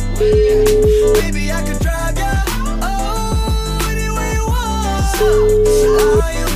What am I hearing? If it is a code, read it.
Ringback tone (Europe, 425 Hz), with noise.